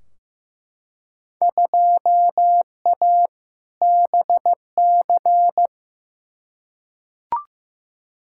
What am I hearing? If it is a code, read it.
Morse code, heard as a keyed tone, about 15 wpm, 700 Hz. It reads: 2A BC
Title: Field Day Exchanges - 15wpm